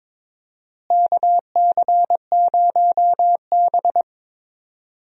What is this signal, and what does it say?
Morse code, heard as a keyed tone, about 22 wpm, 700 Hz. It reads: KC0B
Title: USA Call Signs - 4 Characters - Repeated 2x - 22wpm